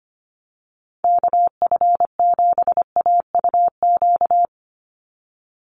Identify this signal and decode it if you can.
Morse code, heard as a keyed tone, about 25 wpm, 690 Hz. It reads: KF7AUQ
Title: USA Call Signs 25wpm